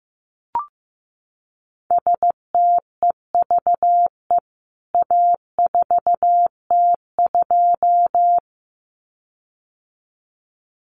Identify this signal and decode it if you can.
Morse code, heard as a keyed tone, about 15 wpm, 700 Hz. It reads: STEVE A4T2
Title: CWT Exchange 15wpm